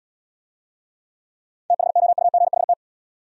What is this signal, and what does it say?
Morse code, heard as a keyed tone, about 78 wpm, 700 Hz. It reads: THOUGHT